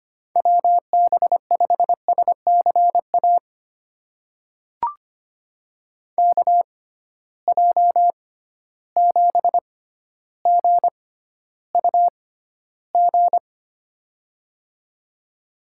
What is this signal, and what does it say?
Morse code, heard as a keyed tone, about 25 wpm, spaced as for 13 wpm, 700 Hz. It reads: WB5SCA KJ7GUG
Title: USA Call Signs - 6 Characters - 6x Character Spacing - 25wpm